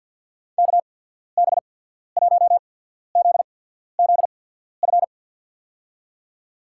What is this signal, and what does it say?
Morse code, heard as a keyed tone, about 53 wpm, spaced as for 14 wpm, 690 Hz. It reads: KB1ZZF
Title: USA Call Signs - 6 Characters - 8x Character Spacing - 50wpm